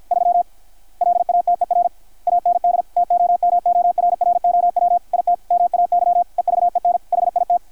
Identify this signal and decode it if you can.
Morse code, heard as a keyed tone, about 53 wpm, 720 Hz, with radio noise. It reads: 3 LATER AND TOMORROW IT MAY EVEN HIT
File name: archived / CRSnet